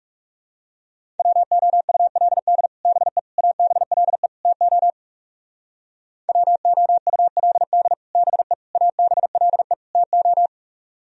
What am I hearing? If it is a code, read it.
Morse code, heard as a keyed tone, about 37 wpm, 700 Hz. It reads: WOULD BE ABLE TO WOULD BE ABLE TO
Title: Speed-Racing - Sets of 4 Words 30wpm